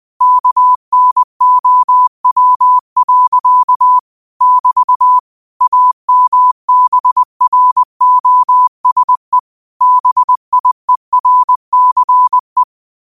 Morse code, heard as a keyed tone, about 20 wpm, 1000 Hz. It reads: KNOW. <BT> AMBROSE BIERCE